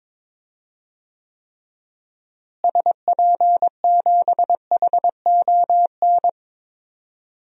Morse code, heard as a keyed tone, about 22 wpm, 700 Hz. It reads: SP7HON
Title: DXCC-100 Call Signs - 6 Characters - Repeated 2x - 22wpm